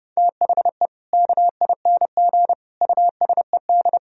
Morse code, heard as a keyed tone, about 30 wpm, 700 Hz. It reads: THE KING USED
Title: Sentences from Top 1000 Words - Review Part 4 - 30wpm